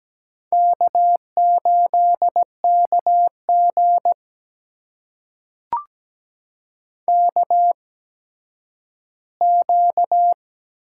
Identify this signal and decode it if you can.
Morse code, heard as a keyed tone, about 17 wpm, spaced as for 10 wpm, 700 Hz. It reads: K8KG KQ
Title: USA Call Signs - 4 Characters - 8x Character Spacing - 17wpm